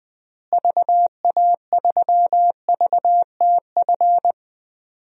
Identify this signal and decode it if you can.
Morse code, heard as a keyed tone, about 20 wpm, 700 Hz. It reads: VA3VTF
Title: DXCC-100 Call Signs - 6 Characters - Repeated 2x - 20wpm